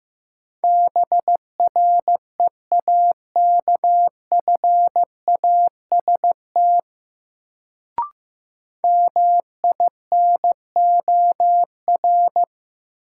Morse code, heard as a keyed tone, about 15 wpm, 700 Hz. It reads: BREAKFAST MINOR
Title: Sentences from Top 1500 Words - New Word Drill - 15wpm